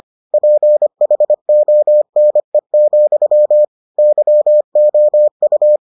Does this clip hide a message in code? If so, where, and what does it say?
Yes: Morse code from 0.3 to 5.8 s, heard as a keyed tone, about 25 wpm, 600 Hz. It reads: PHONE, YOU